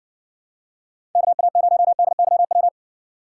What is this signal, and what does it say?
Morse code, heard as a keyed tone, about 60 wpm, 700 Hz. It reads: KA0DYW